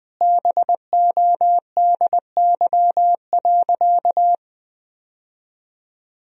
Morse code, heard as a keyed tone, about 20 wpm, 700 Hz. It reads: BODY.
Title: Sentences from Top 1000 Words - Review Part 2 - 20wpm